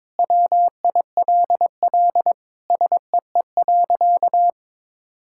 Morse code, heard as a keyed tone, about 22 wpm, 700 Hz. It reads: WILL SEE.
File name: practice / 22wpm / Sentences with 3-Word Phrases - 22wpm